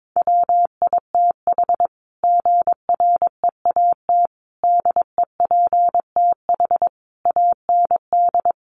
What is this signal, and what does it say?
Morse code, heard as a keyed tone, about 22 wpm, 700 Hz. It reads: WITH GREAT DEPTH AND
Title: Sentences from Top 1200 Words 22wpm